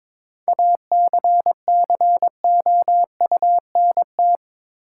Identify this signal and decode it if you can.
Morse code, heard as a keyed tone, about 22 wpm, 700 Hz. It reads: ACCOUNT
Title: Sentences from Top 900 Words - New Word Drill - 22wpm